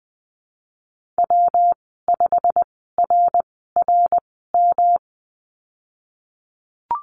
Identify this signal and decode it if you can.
Morse code, heard as a keyed tone, about 20 wpm, spaced as for 15 wpm, 700 Hz. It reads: W5RRM
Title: 122 Lesson 20g - Number 3 - Call Signs 20wpm